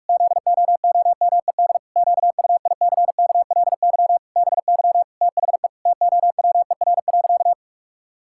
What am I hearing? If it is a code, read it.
Morse code, heard as a keyed tone, about 46 wpm, 700 Hz. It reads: ZOOMED QUICKLY BY THE TOWER.
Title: Pangram Challenge v2 - 45wpm